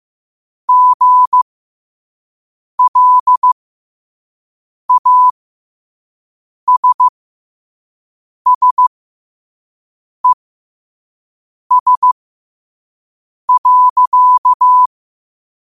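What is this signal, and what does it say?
Morse code, heard as a keyed tone, about 15 wpm, spaced as for 5 wpm, 1000 Hz. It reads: GLASSES.
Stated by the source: Morse code practice at 5 wpm.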